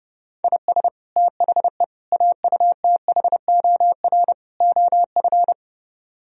Morse code, heard as a keyed tone, about 32 wpm, 700 Hz. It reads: IS THE AUTHOR OF